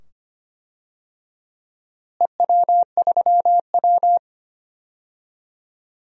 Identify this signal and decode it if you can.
Morse code, heard as a keyed tone, about 25 wpm, 700 Hz. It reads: EW3W
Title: DXCC-100 Call Signs - 4 Characters - Repeated 2x - 25wpm